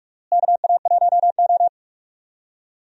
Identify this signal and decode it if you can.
Morse code, heard as a keyed tone, about 46 wpm, 700 Hz. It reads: KA1O
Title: USA Call Signs - 4 Characters - Repeated 3x - 45wpm